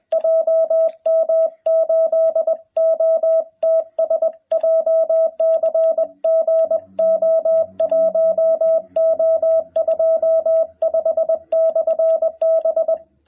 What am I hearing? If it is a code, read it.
Morse code, heard as a keyed tone, about 21 wpm, 650 Hz. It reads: JM8OTS JCGO1O25/B
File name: jh8cbh / 2021cw